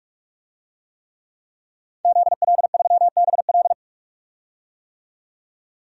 Morse code, heard as a keyed tone, about 46 wpm, 700 Hz. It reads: ZL3BL
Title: DXCC-100 Call Signs - 5 Characters - 45wpm